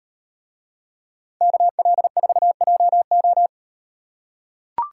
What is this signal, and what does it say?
Morse code, heard as a keyed tone, about 38 wpm, 700 Hz. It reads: KL4JO